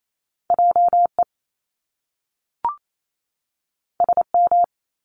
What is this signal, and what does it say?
Morse code, heard as a keyed tone, about 28 wpm, 700 Hz. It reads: JE SM
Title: Top Two Letter Combinations in Words 28wpm